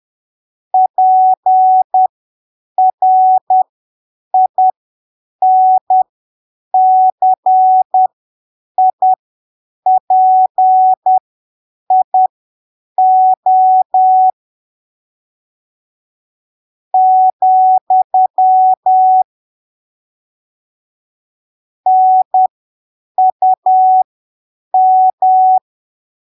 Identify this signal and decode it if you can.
Morse code, heard as a keyed tone, about 10 wpm, spaced as for 7 wpm, 750 Hz. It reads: PRINCIPIO , NUM
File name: arquivos / texto / texto 11 10ppm